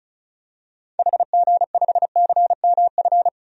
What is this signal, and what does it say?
Morse code, heard as a keyed tone, about 34 wpm, 700 Hz. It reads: HG5CMF
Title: DXCC-100 Call Signs - 6 Characters - 35wpm